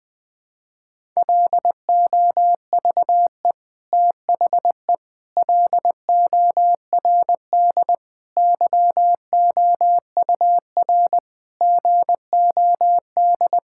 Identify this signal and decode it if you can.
Morse code, heard as a keyed tone, about 20 wpm, 700 Hz. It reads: LOVE THE LORD YOUR GOD
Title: Sets of 9 Words - 20wpm